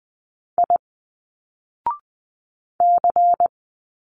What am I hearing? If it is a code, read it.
Morse code, heard as a keyed tone, about 20 wpm, 700 Hz. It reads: I C
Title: Single Letters 20wpm